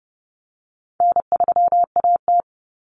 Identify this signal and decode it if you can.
Morse code, heard as a keyed tone, about 30 wpm, 700 Hz. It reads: N3AT
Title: Common Cut Numbers - 4-Digit 30wpm